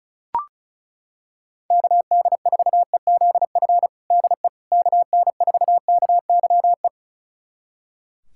Morse code, heard as a keyed tone, about 34 wpm, 700 Hz. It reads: KD4EZF DE KN4KYE
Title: USA Callsign DE Callsign 35wpm